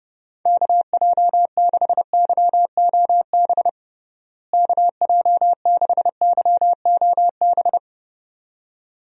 Morse code, heard as a keyed tone, about 32 wpm, 700 Hz. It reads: KJ6YOB KJ6YOB
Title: USA Call Signs - 6 Characters - Repeated 2x - 30wpm